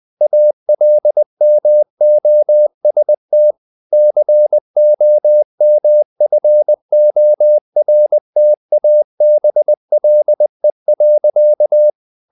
Morse code, heard as a keyed tone, about 20 wpm, 600 Hz. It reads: ALMOST COMFORTABLE.
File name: the old man and the sea / the old man and the sea part 1 20wpm